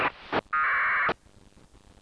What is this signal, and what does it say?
Packet radio (AX.25, as in APRS).
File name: satellite / raft / raft psk31 pe1itr 28jan2007 0622UTC